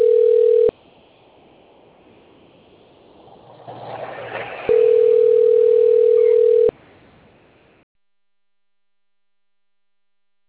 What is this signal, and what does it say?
Ringback tone (North America), 440 + 480 Hz.
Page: 177.136.239.49 Marilia 2